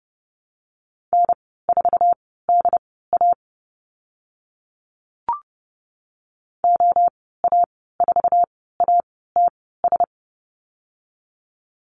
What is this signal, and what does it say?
Morse code, heard as a keyed tone, about 31 wpm, spaced as for 17 wpm, 700 Hz. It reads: N4DA OA4ATS
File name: Lesson 15g - Number 5 - Call Signs 30wpm